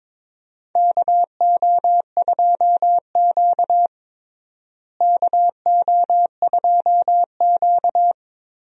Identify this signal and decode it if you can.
Morse code, heard as a keyed tone, about 22 wpm, 700 Hz. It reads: KO2Q KO2Q
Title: USA Call Signs - 4 Characters - Repeated 2x - 22wpm